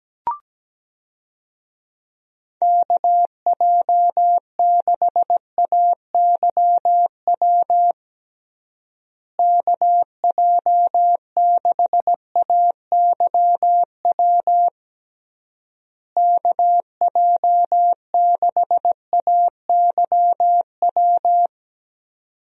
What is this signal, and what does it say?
Morse code, heard as a keyed tone, about 17 wpm, 700 Hz. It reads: KJ6AYW KJ6AYW KJ6AYW
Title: USA Call Signs - 6 Characters - Repeated 3x - 17wpm